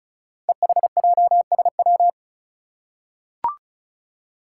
Morse code, heard as a keyed tone, about 36 wpm, 700 Hz. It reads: EHJSW
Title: Code Groups - 5 Letters - 35wpm